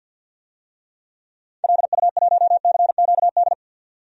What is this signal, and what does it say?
Morse code, heard as a keyed tone, about 49 wpm, 700 Hz. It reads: LU1CQD
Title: DXCC-100 Call Signs - 6 Characters - 50wpm